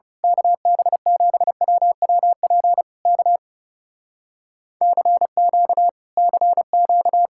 Morse code, heard as a keyed tone, about 28 wpm, 700 Hz. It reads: KB7WWP K CQ CQ
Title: Speed-Racing Common QSO Phrases - 30wpm